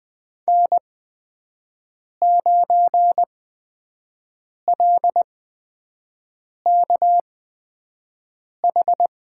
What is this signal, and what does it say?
Morse code, heard as a keyed tone, about 20 wpm, spaced as for 5 wpm, 700 Hz. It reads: N9LKH
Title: USA Call Signs - 5 Characters - 8x Character Spacing - 20wpm